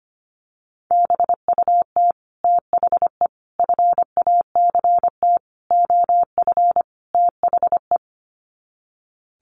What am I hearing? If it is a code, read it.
Morse code, heard as a keyed tone, about 25 wpm, 700 Hz. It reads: BUT THE FACT OF THE
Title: Sets of 5 Words Encore2 25wpm